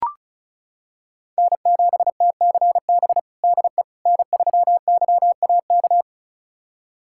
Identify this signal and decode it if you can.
Morse code, heard as a keyed tone, about 36 wpm, 700 Hz. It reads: N7TCB DE N3YAK
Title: USA Callsign DE Callsign - 5 Characters - 35wpm